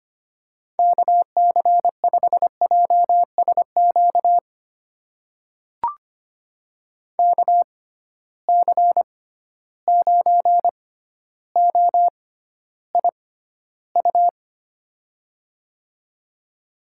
Morse code, heard as a keyed tone, about 25 wpm, spaced as for 13 wpm, 700 Hz. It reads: KC5JSQ KC9OIU
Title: USA Call Signs - 6x Character Spacing - 25wpm